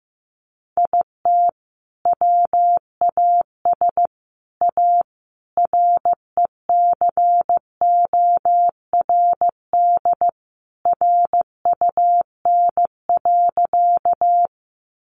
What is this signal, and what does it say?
Morse code, heard as a keyed tone, about 15 wpm, 700 Hz. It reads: IT WAS A RECORD RUN.
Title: Sentences from Top 500 Words 15wpm